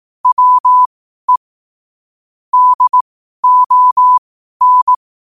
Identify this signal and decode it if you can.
Morse code, heard as a keyed tone, about 18 wpm, spaced as for 12 wpm, 1000 Hz. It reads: WE DON
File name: qotdfiles / 12wpm